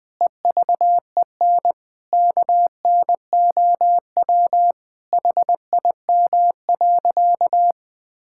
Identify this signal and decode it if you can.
Morse code, heard as a keyed tone, about 21 wpm, 700 Hz. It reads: EVEN KNOW HIM.
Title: Sentences with 4-Word Phrases - Spoken Beginning - 20wpm